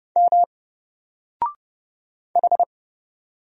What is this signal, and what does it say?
Morse code, heard as a keyed tone, about 30 wpm, 700 Hz. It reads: M H
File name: Single-Letter-Number-30wpm